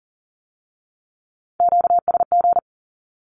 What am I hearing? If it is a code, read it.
Morse code, heard as a keyed tone, about 42 wpm, 700 Hz. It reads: QSG
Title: Q-Codes with Spoken Meaning - 40wpm